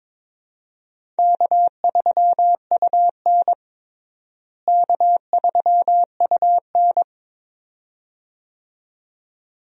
Morse code, heard as a keyed tone, about 21 wpm, 700 Hz. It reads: K3UN K3UN
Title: USA Call Signs - 4 Characters - Repeated 2x - 22wpm